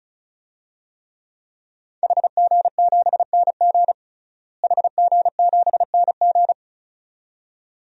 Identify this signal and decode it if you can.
Morse code, heard as a keyed tone, about 36 wpm, 700 Hz. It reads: HG7NG HG7NG
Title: DXCC-100 Call Signs - 5 Characters - Repeated 2x - 35wpm